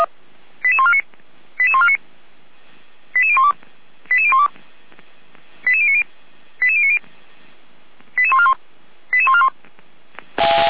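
Five-tone selective call (ZVEI), heard as quick runs of five tones, tones of about 70 ms. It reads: 88149 88149 88012 88012 88090 88090 88141 88141
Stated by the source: ZVEI five-tone call received by SDR, from feuerwehr-orlamuende.org archive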